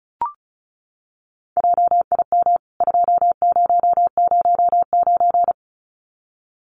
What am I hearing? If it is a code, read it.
Morse code, heard as a keyed tone, about 34 wpm, 700 Hz. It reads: JIM 2009
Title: CWT Exchange 35wpm